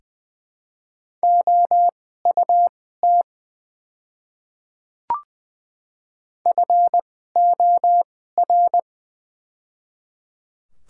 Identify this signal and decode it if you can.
Morse code, heard as a keyed tone, about 20 wpm, spaced as for 14 wpm, 700 Hz. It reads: OUT FOR
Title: 2x Character Spacing - Top 100 Words - 20wpm